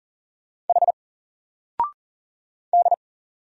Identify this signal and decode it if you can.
Morse code, heard as a keyed tone, about 40 wpm, 700 Hz. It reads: H D
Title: Single Letter-Number Alt 40wpm